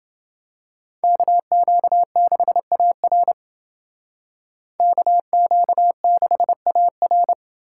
Morse code, heard as a keyed tone, about 28 wpm, 700 Hz. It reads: KQ6AR KQ6AR